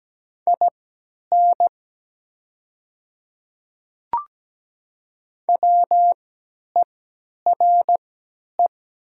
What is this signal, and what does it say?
Morse code, heard as a keyed tone, about 17 wpm, spaced as for 10 wpm, 700 Hz. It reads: IN WERE
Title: 3x Character Spacing - Top 300 Words - 17wpm